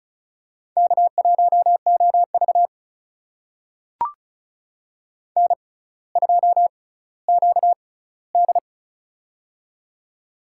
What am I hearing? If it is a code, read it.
Morse code, heard as a keyed tone, about 36 wpm, spaced as for 18 wpm, 700 Hz. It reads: K1OV N2QD
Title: USA Call Signs - 4 Characters - 6x Character Spacing - 35wpm